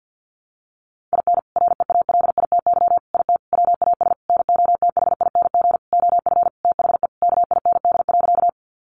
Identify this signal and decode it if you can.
Morse code, heard as a keyed tone, about 48 wpm, 700 Hz. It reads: IN REALITY IT WAS NOTHING OF THE KIND.
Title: Sentences from Top 1200 Words 50wpm